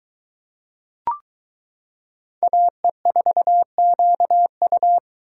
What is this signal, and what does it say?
Morse code, heard as a keyed tone, about 22 wpm, 700 Hz. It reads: AE4QU